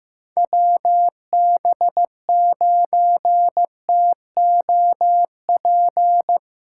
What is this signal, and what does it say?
Morse code, heard as a keyed tone, about 16 wpm, 700 Hz. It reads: WB9TOP
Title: USA Callsign DE Callsign 15wpm